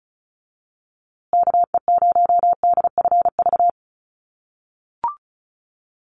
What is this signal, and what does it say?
Morse code, heard as a keyed tone, about 34 wpm, 700 Hz. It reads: KE0DFV
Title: USA Call Signs Repeated 3x 35wpm